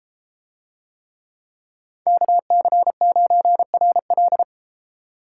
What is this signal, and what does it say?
Morse code, heard as a keyed tone, about 32 wpm, 700 Hz. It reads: KC9RL